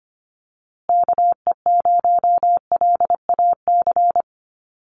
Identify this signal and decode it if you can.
Morse code, heard as a keyed tone, about 25 wpm, 700 Hz. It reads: KE0LAC
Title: USA Call Signs Repeated 3x 25wpm